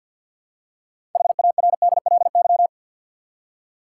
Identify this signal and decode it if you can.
Morse code, heard as a keyed tone, about 52 wpm, 700 Hz. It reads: HARDLY
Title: Six Letter Words - 50wpm